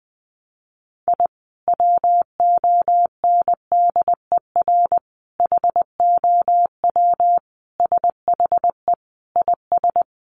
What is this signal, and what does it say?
Morse code, heard as a keyed tone, about 20 wpm, 700 Hz. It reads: I WONDER HOW SHE IS
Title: Sentences from Top 1500 Words 20wpm